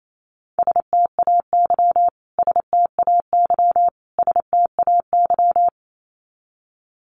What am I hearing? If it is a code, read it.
Morse code, heard as a keyed tone, about 28 wpm, 700 Hz. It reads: STAY STAY STAY
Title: Top 500 Words Repeated 28wpm